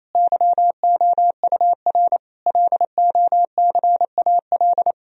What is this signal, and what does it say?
Morse code, heard as a keyed tone, about 28 wpm, 700 Hz. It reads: YOUR LOCAL
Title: Sentences from Top 1000 Words - Review Part 3 - 28wpm